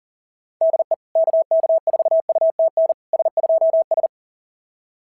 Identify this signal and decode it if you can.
Morse code, heard as a keyed tone, about 40 wpm, 690 Hz. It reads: DE KK4UTN S2S
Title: SOTA QSO Practice 40wpm